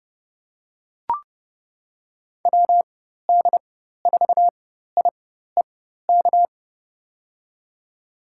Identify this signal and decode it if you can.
Morse code, heard as a keyed tone, about 32 wpm, spaced as for 14 wpm, 700 Hz. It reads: WD4IEK
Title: USA Call Signs - 6 Characters - 4x Character Spacing - 30wpm